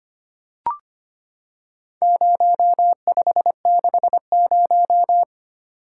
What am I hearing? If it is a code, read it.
Morse code, heard as a keyed tone, about 25 wpm, 700 Hz. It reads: 0560
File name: practice / 25wpm / Four Letter-Number Combinations - 25wpm